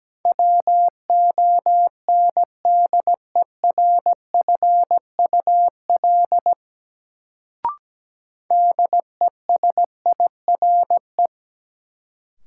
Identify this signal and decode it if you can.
Morse code, heard as a keyed tone, about 17 wpm, 700 Hz. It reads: WONDERFUL DESIRE
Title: Sentences from Top 1100 Words - New Word Drill Alt - 17wpm